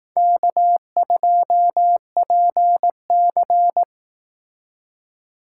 Morse code, heard as a keyed tone, about 18 wpm, 700 Hz. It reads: K2PC
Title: Speed Racing - USA 1x2 Call Signs 15wpm